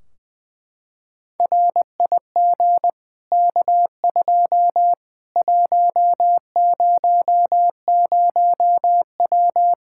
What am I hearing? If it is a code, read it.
Morse code, heard as a keyed tone, about 20 wpm, 700 Hz. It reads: RIG K2 100W
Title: K1USN SST QSO Practice 20wpm